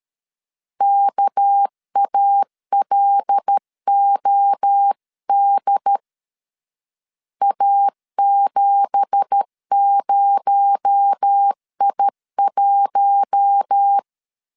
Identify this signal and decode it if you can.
Morse code, heard as a keyed tone, about 13 wpm, 800 Hz. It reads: KALOD A70I1
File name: LEZIONI / 9 2867 12wpm 2322